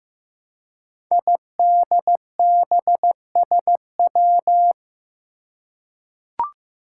Morse code, heard as a keyed tone, about 15 wpm, 700 Hz. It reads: IDBSW
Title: Five Challenging Letter-Number Combinations 15wpm